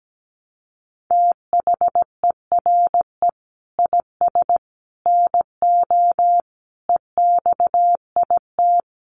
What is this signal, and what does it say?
Morse code, heard as a keyed tone, about 17 wpm, 700 Hz. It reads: THERE IS NO EXIT
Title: Sentences from Top 1100 Words 17wpm